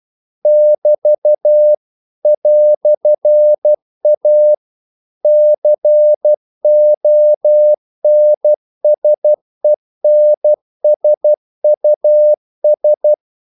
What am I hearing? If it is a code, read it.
Morse code, heard as a keyed tone, about 12 wpm, 600 Hz. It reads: <BT> "A CONSENSUS
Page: English quotations in CW (telegraphy - morse)